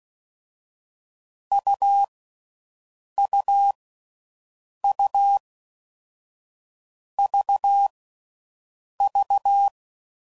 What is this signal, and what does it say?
Morse code, heard as a keyed tone, about 16 wpm, 800 Hz. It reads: U U U V V
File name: Difficult Combos